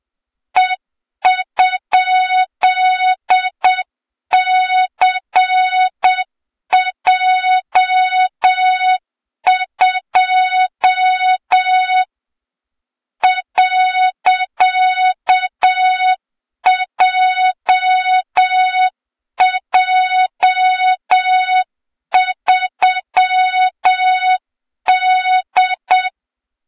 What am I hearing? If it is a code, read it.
Morse code, heard as a keyed tone, about 7 wpm, 3700 Hz, with noise. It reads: E?CJ2 .JJ3D
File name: Morze-HA2MN / hang / 23 GYAKORLAT